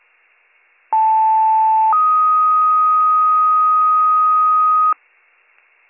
Two-tone page, heard as a one-second tone, then a three-second tone, about 880 then 1250 Hz. It is heard from 0.9 to 4.9 s.